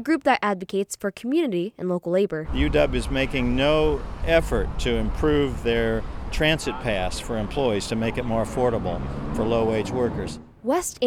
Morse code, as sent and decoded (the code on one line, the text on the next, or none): none